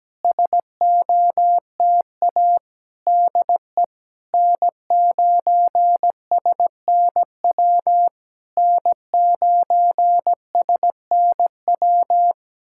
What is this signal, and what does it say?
Morse code, heard as a keyed tone, about 17 wpm, 700 Hz. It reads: SOTA DE N9SNW N9SNW
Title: SOTA QSO Practice 17wpm